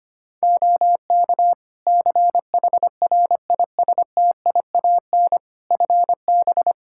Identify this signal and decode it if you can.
Morse code, heard as a keyed tone, about 25 wpm, 700 Hz. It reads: OK CHRISTIAN FB
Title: Practice QSO - Exchange 2 - Info - 25wpm